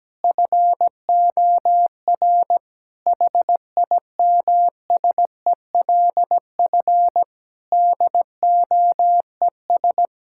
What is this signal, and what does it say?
Morse code, heard as a keyed tone, about 17 wpm, 700 Hz. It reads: FOR HIMSELF DOES
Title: Sentences from Top 1600 Words 17wpm